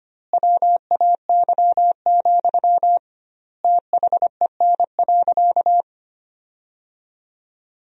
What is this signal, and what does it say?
Morse code, heard as a keyed tone, about 25 wpm, 700 Hz. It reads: WAY, THEN.
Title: 2x Word-Spacing - Sentences from Top 400 Words - 25wpm